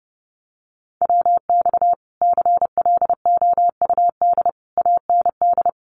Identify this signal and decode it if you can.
Morse code, heard as a keyed tone, about 30 wpm, 700 Hz. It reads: WX CLOUD AND
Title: Common QSO Phrases Encore 30wpm